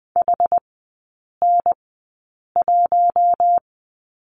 Morse code, heard as a keyed tone, about 20 wpm, spaced as for 8 wpm, 700 Hz. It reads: HN1
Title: 121 Lesson 20e - Number 3 - Three Characters 20wpm